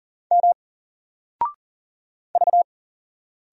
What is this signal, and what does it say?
Morse code, heard as a keyed tone, about 40 wpm, 700 Hz. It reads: M V
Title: Single Letter-Number Alt 40wpm